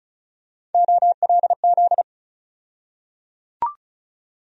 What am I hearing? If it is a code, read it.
Morse code, heard as a keyed tone, about 36 wpm, 700 Hz. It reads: OLZ
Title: USA Call Sign Postfixes - 3 Letters 35wpm